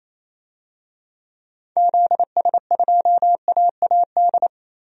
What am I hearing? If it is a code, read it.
Morse code, heard as a keyed tone, about 28 wpm, 700 Hz. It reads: ZS2AAD
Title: DXCC-100 Call Signs - 6 Characters - 28wpm